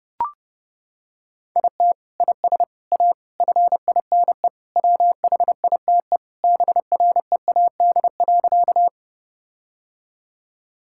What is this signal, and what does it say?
Morse code, heard as a keyed tone, about 32 wpm, 700 Hz. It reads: IT IS A FINE WHITE BREAD.
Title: Sentences from Top 1600 Words 30wpm